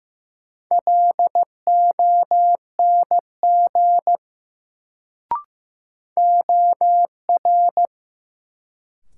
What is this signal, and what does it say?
Morse code, heard as a keyed tone, about 15 wpm, 700 Hz. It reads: LONG OR
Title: Sentences from Top 100 Words - New Word Drill - 15wpm